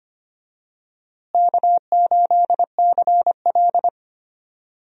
Morse code, heard as a keyed tone, about 25 wpm, 700 Hz. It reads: K8CL
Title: USA Call Signs - 4 Characters - Repeated 2x - 25wpm